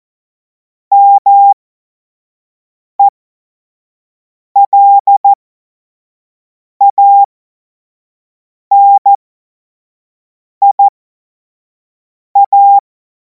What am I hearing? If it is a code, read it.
Morse code, heard as a keyed tone, about 14 wpm, spaced as for 5 wpm, 800 Hz. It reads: MELANIA
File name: Morsekurs / Witze / Witz 12. Mai 2026 25 F70